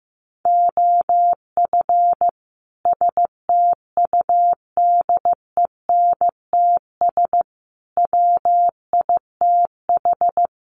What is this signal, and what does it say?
Morse code, heard as a keyed tone, about 15 wpm, 700 Hz. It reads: OF STUDENTS WITH
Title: Sets of 5 Words Encore 15wpm